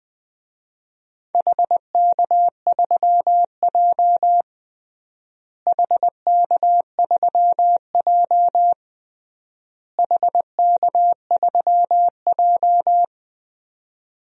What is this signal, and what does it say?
Morse code, heard as a keyed tone, about 21 wpm, 700 Hz. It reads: HK3J HK3J HK3J
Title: DXCC-100 Call Signs - 4 Characters - Repeated 3x - 20wpm